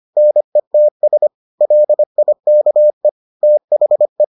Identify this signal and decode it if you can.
Morse code, heard as a keyed tone, about 25 wpm, 600 Hz. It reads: NETS LIKE THE